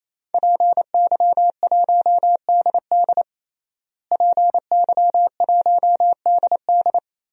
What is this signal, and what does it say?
Morse code, heard as a keyed tone, about 28 wpm, 690 Hz. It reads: PY1DD PY1DD
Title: DXCC-100 Call Signs - 5 Characters - Repeated 3x - 28wpm